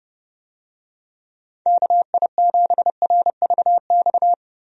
Morse code, heard as a keyed tone, about 30 wpm, 700 Hz. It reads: KI7RVX
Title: USA Call Signs - 6 Characters - Repeated 2x - 30wpm